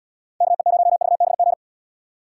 Morse code, heard as a keyed tone, about 68 wpm, 700 Hz. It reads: KE0VBG